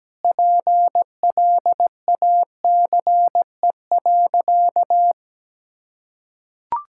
Morse code, heard as a keyed tone, about 17 wpm, 700 Hz. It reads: PLACE.
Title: Sentences from Top 1000 Words - Review Part 5 - 17wpm